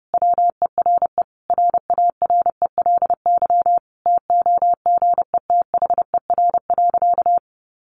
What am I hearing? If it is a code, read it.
Morse code, heard as a keyed tone, about 30 wpm, 700 Hz. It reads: WERE RARELY TOGETHER.
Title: Sentences from Top 1400 Words 30wpm